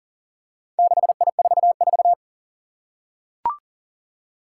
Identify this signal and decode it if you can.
Morse code, heard as a keyed tone, about 42 wpm, 700 Hz. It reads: 6I44
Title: Four Challenging Letter-Number Combinations 40wpm